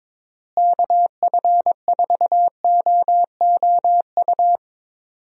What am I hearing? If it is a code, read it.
Morse code, heard as a keyed tone, about 22 wpm, 700 Hz. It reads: KF4OOU
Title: USA Call Signs - 6 Characters - Repeated 2x - 22wpm